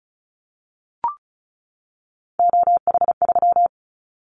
Morse code, heard as a keyed tone, about 36 wpm, 700 Hz. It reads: OH3
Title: Callsign Prefixes for Top 50 DXCC - 35wpm